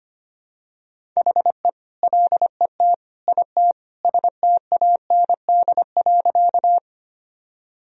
Morse code, heard as a keyed tone, about 25 wpm, 700 Hz. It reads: HE LET IT STAND.
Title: Sentences from Top 1000 Words - Review Part 1 - 25wpm